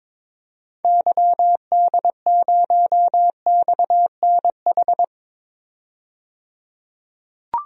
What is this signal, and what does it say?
Morse code, heard as a keyed tone, about 21 wpm, 700 Hz. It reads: YD0XNH
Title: DXCC-100 Call Signs - 6 Characters - 22wpm